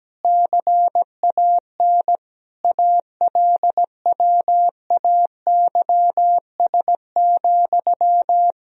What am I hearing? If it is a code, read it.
Morse code, heard as a keyed tone, about 17 wpm, 700 Hz. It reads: CAN ALWAYS,
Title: Sentences from Top 1000 Words - Review Part 4 - 17wpm